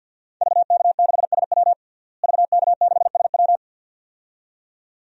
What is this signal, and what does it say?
Morse code, heard as a keyed tone, about 52 wpm, 700 Hz. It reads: VK6SW VK6SW
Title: DXCC-100 Call Signs - 5 Characters - Repeated 2x - 50wpm